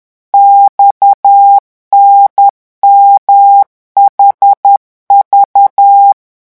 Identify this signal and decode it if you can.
Morse code, heard as a keyed tone, about 10 wpm, 800 Hz. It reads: XNMHV